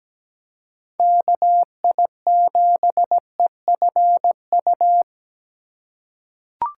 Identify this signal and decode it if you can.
Morse code, heard as a keyed tone, about 17 wpm, 700 Hz. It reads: KI7EFU
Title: USA Call Signs - 6 Characters - 2x Character Spacing - 17wpm